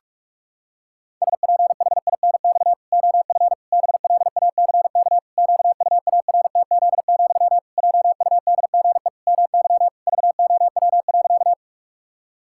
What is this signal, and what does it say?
Morse code, heard as a keyed tone, about 46 wpm, 700 Hz. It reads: SPHINX OF BLACK QUARTZ, JUDGE MY VOW.
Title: Pangram Challenge v2 - 45wpm